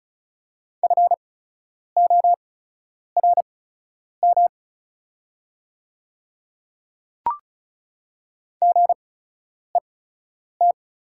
Morse code, heard as a keyed tone, about 36 wpm, spaced as for 10 wpm, 700 Hz. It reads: FORM GET
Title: 8x Character Spacing - Top 200 Words - 35wpm